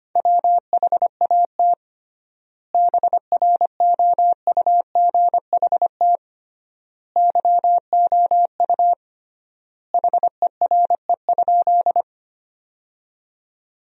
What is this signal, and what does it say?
Morse code, heard as a keyed tone, about 25 wpm, 700 Hz. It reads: WHAT BROUGHT YOU HERE?
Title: 3x Word-Spacing - Sentences from Top 500 Words - 25wpm